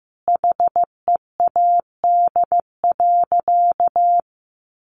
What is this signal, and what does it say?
Morse code, heard as a keyed tone, about 15 wpm, 700 Hz. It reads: HEAD.
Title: Sentences from Top 300 Words 15wpm